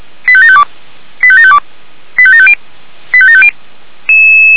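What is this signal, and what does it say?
Five-tone selective call (ZVEI), heard as quick runs of five tones, tones of about 70 ms. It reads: 85752 85752 85750 85750
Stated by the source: ZVEI five-tone call 89485 received by SDR.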